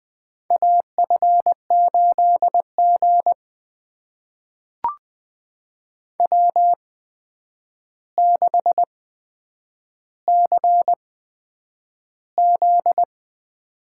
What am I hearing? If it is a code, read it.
Morse code, heard as a keyed tone, about 20 wpm, spaced as for 9 wpm, 700 Hz. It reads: AF8G W6CZ
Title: USA Call Signs - 4 Characters - 8x Character Spacing - 20wpm